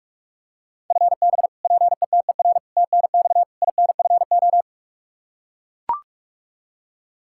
Morse code, heard as a keyed tone, about 41 wpm, 700 Hz. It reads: FB PETER TNX INFO
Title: Practice QSO - Exchange 3 - More Info - 45wpm